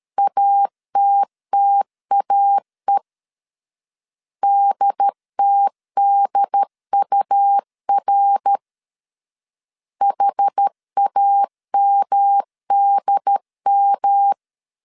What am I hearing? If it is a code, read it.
Morse code, heard as a keyed tone, about 13 wpm, 800 Hz. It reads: ATTAE DTDUR HAMDM
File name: CW / DET / 1 3 1500C S C 12wpm 2750